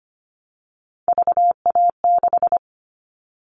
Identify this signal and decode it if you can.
Morse code, heard as a keyed tone, about 25 wpm, 700 Hz. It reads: VA6
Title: Callsign Prefixes for Top 25 DXCC - Repeated 3x - 25wpm